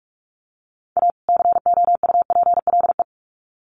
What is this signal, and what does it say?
Morse code, heard as a keyed tone, about 44 wpm, 700 Hz. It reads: A COUPLE
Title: Sets of 2 Words 45wpm